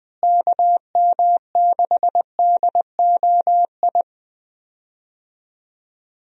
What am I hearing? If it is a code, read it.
Morse code, heard as a keyed tone, about 20 wpm, 700 Hz. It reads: KM6DOI
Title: USA Call Signs - 6 Characters - Repeated 3x - 20wpm